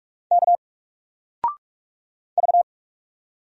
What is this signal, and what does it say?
Morse code, heard as a keyed tone, about 44 wpm, 700 Hz. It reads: K V
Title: Single Character (Letter:Number:Punctuation:Prosign) 45wpm